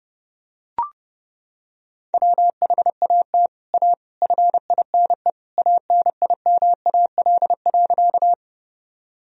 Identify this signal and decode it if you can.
Morse code, heard as a keyed tone, about 30 wpm, 700 Hz. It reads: WHAT A FINE ANIMAL.
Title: Sentences from Top 1000 Words - Review Part 1 - 30wpm